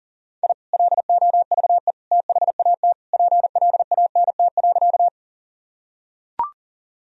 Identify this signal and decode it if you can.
Morse code, heard as a keyed tone, about 40 wpm, 700 Hz. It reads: I LOVE THAT PLANT.
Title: Sentences from Top 1000 Words - Review Part 1 - 40wpm